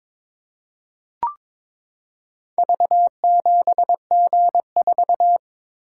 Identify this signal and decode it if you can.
Morse code, heard as a keyed tone, about 22 wpm, 700 Hz. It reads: V7G4
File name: practice / 22wpm / Four Letter-Number Combinations - 22wpm